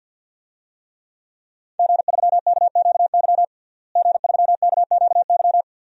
Morse code, heard as a keyed tone, about 52 wpm, 700 Hz. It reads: G3KQY G3KQY
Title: DXCC-100 Call Signs - 5 Characters - Repeated 3x - 50wpm